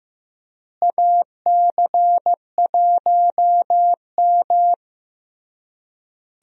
Morse code, heard as a keyed tone, about 16 wpm, 700 Hz. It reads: AC1M
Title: USA Call Signs - 4 Characters - Repeated 3x - 15wpm